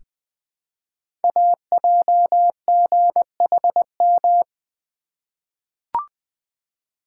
Morse code, heard as a keyed tone, about 20 wpm, 700 Hz. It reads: AJGHM
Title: Code Groups - 5 Letters - 20wpm